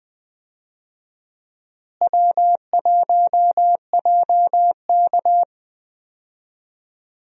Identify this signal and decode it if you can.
Morse code, heard as a keyed tone, about 21 wpm, 700 Hz. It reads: W1JK
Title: USA Call Signs - 4 Characters - Repeated 3x - 20wpm